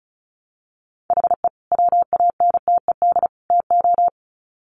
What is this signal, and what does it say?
Morse code, heard as a keyed tone, about 34 wpm, 700 Hz. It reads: HE WANTED TO
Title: Sets of 3 Words 35wpm